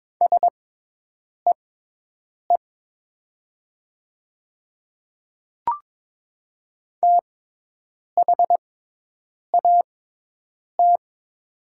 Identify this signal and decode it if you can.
Morse code, heard as a keyed tone, about 22 wpm, spaced as for 8 wpm, 700 Hz. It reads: SEE THAT
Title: 6x Character Spacing - Top 100 Words - 22wpm